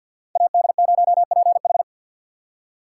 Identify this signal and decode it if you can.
Morse code, heard as a keyed tone, about 62 wpm, 700 Hz. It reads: AD0PH
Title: Speed Racing - USA 2x2 Call Signs 50wpm